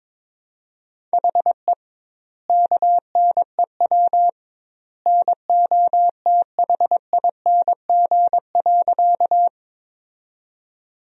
Morse code, heard as a keyed tone, about 22 wpm, 700 Hz. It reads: HE KNEW NOTHING.
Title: 2x Word-Spacing - Sentences from Top 500 Words - 22wpm